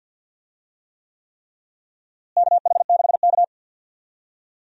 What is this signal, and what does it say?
Morse code, heard as a keyed tone, about 51 wpm, 700 Hz. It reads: KH6K
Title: DXCC-100 Call Signs - 4 Characters - Repeated 3x - 50wpm